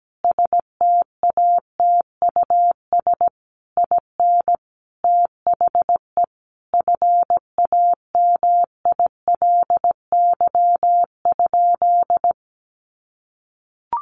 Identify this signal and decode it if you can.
Morse code, heard as a keyed tone, about 17 wpm, 700 Hz. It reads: STATUS IN THE FAMILY?
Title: Sentences from Top 1400 Words 17wpm